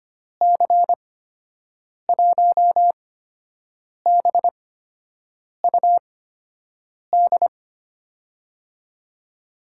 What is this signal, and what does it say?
Morse code, heard as a keyed tone, about 25 wpm, spaced as for 7 wpm, 700 Hz. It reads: C1BUD
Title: USA Call Signs - 6 Characters - 8x Character Spacing - 25wpm